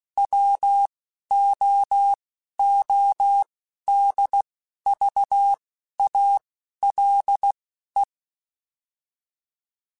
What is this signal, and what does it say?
Morse code, heard as a keyed tone, about 16 wpm, spaced as for 11 wpm, 800 Hz. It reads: WOODVALE